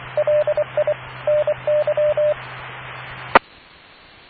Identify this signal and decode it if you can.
Morse code, heard as a keyed tone, about 24 wpm, 620 Hz. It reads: LI NY E